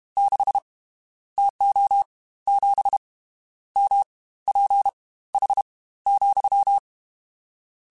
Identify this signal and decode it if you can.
Morse code, heard as a keyed tone, about 32 wpm, spaced as for 17 wpm, 800 Hz. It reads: 6 TO 7 MPH,
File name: Spaced Characters / Spaced Characters-mp3s / Weather 4 24 sp @30